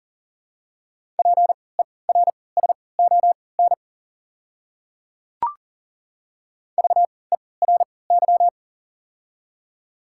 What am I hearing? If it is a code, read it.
Morse code, heard as a keyed tone, about 40 wpm, spaced as for 23 wpm, 700 Hz. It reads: PERSON VERY
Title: 3x Character Spacing - Top 500 Words - 40wpm